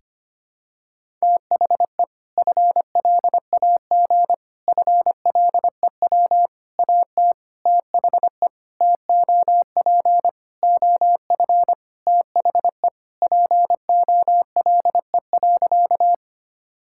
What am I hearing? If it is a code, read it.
Morse code, heard as a keyed tone, about 25 wpm, 700 Hz. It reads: THE FLAG FLEW AT THE TOP OF THE POLE.
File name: practice / 25wpm / Sentences with 4-Word Phrases - 25wpm